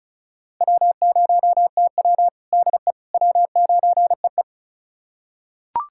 Morse code, heard as a keyed tone, about 34 wpm, 700 Hz. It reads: W0TW DE W9EE
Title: USA Callsign DE Callsign - 4 Characters - 35wpm